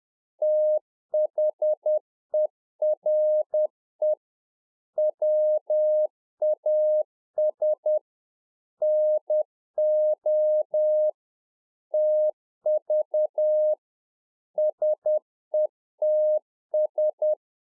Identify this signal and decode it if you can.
Morse code, heard as a keyed tone, about 10 wpm, 610 Hz. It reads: THERE WAS NO TV SETS